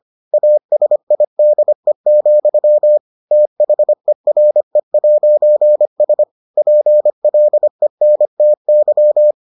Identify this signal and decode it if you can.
Morse code, heard as a keyed tone, about 25 wpm, 600 Hz. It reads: ASIDE, THERE'S PLENTY